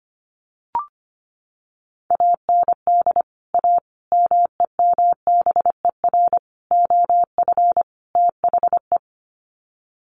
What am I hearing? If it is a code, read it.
Morse code, heard as a keyed tone, about 25 wpm, 700 Hz. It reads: AND A MEMBER OF THE
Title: Sets of 5 Words 25wpm